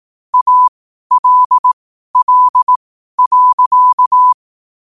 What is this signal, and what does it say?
Morse code, heard as a keyed tone, about 18 wpm, spaced as for 12 wpm, 1000 Hz. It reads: ALL.
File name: qotdfiles / 12wpm